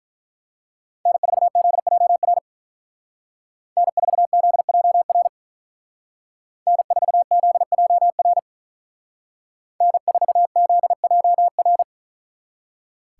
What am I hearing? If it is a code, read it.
Morse code, heard as a keyed tone, about 43 wpm, 700 Hz. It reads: N4ZJR N4ZJR N4ZJR N4ZJR
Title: Speed Racing - USA 1x3 Call Signs 35wpm